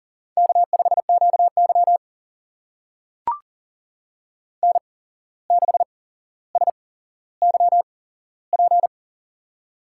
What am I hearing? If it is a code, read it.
Morse code, heard as a keyed tone, about 40 wpm, spaced as for 16 wpm, 700 Hz. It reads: K5QY N6SYP
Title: USA Call Signs - 8x Character Spacing - 40wpm